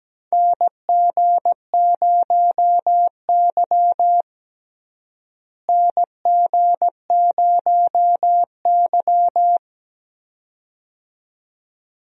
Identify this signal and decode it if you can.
Morse code, heard as a keyed tone, about 17 wpm, 700 Hz. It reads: NG0Y NG0Y
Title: USA Call Signs - 4 Characters - Repeated 2x - 17wpm